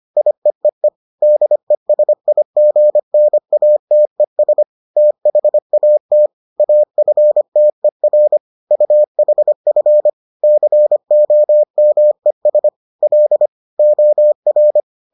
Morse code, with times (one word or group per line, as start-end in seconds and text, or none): 0.2-0.9 IEEE
1.2-4.6 DESIGNATES
5.0-6.3 THAT
6.6-8.4 AFTER
8.7-10.1 UHF
10.4-12.7 COMES
13.0-13.5 L
13.8-14.8 OR